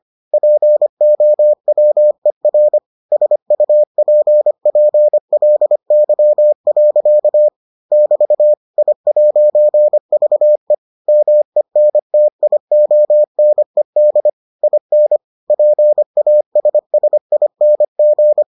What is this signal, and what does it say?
Morse code, heard as a keyed tone, about 25 wpm, 600 Hz. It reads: POWER SUPPLY. <BT> I'VE MENTIONED IN PASSING